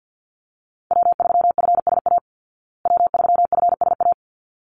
Morse code, heard as a keyed tone, about 50 wpm, 700 Hz. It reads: W3FSA W3FSA
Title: Call Signs Repeated 3x - 50wpm